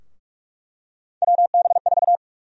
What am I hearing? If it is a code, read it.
Morse code, heard as a keyed tone, about 48 wpm, 700 Hz. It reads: WB4
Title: USA Call Sign Prefixes - 45wpm